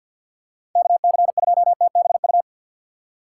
Morse code, heard as a keyed tone, about 47 wpm, 700 Hz. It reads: KC2TBU
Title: USA Call Signs - 6 Characters - 6x Character Spacing - 50wpm